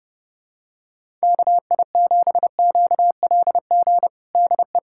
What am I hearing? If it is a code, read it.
Morse code, heard as a keyed tone, about 30 wpm, 700 Hz. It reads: KI7QLG DE
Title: USA Callsign DE Callsign - 6 Characters - 30wpm